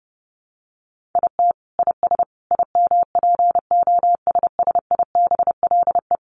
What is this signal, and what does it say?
Morse code, heard as a keyed tone, about 30 wpm, 700 Hz. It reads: IT IS IMPOSSIBLE
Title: Sentences from Top 1100 Words 30wpm